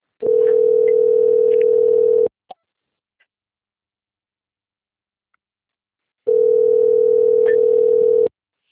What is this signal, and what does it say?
Ringback tone (North America), 440 + 480 Hz.